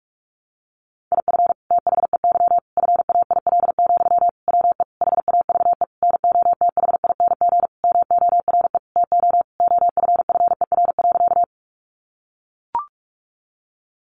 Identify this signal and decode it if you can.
Morse code, heard as a keyed tone, about 44 wpm, 700 Hz. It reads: IF THEY FAIL, WE HAVE NOTHING MORE TO OFFER.
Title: Sentences from Top 1100 Words 45wpm